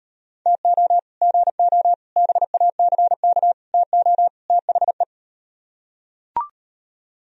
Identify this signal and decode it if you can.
Morse code, heard as a keyed tone, about 38 wpm, 700 Hz. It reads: TO GO BACK TO THE